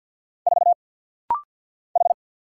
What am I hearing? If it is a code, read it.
Morse code, heard as a keyed tone, about 49 wpm, 700 Hz. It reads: 4 H
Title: Single Letter-Number - Rapid-Fire 50wpm